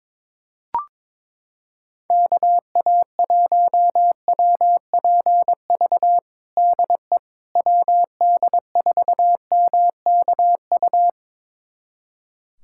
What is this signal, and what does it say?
Morse code, heard as a keyed tone, about 22 wpm, 700 Hz. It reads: KA1WPV DE WD4MKU
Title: USA Callsign DE Callsign - 6 Characters - 22wpm